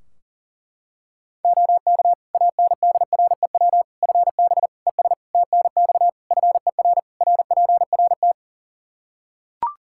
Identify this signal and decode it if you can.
Morse code, heard as a keyed tone, about 37 wpm, 700 Hz. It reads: OK ANDREW FB ES TNX FER RPRT
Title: Practice QSO - Exchange 2 - Info - 40wpm